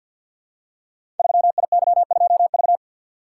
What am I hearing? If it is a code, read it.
Morse code, heard as a keyed tone, about 51 wpm, 700 Hz. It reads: 3IY2V
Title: Code Groups - 5 Letter:Numbers - 50wpm